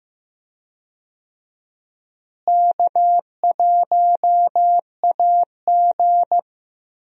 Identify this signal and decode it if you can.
Morse code, heard as a keyed tone, about 15 wpm, 700 Hz. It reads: K1AG
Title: USA Call Signs - 4 Characters - Repeated 3x - 15wpm